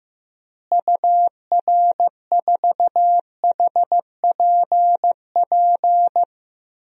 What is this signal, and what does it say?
Morse code, heard as a keyed tone, about 15 wpm, 700 Hz. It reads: UR4HPP
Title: DXCC-100 Call Signs - 6 Characters - Repeated 3x - 15wpm